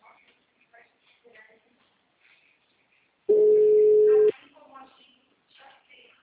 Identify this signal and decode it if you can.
Ringback tone (Europe, 425 Hz), with speech.